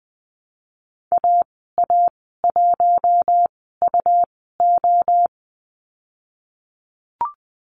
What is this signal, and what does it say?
Morse code, heard as a keyed tone, about 20 wpm, spaced as for 15 wpm, 700 Hz. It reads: AA1UO
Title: 122 Lesson 20g - Number 3 - Call Signs 20wpm